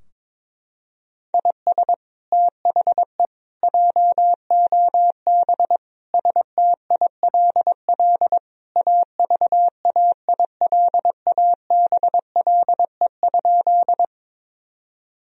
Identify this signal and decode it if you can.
Morse code, heard as a keyed tone, about 22 wpm, 700 Hz. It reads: IS THE JOB STILL AVAILABLE?
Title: Sentences from Top 1000 Words - Review Part 3 - 22wpm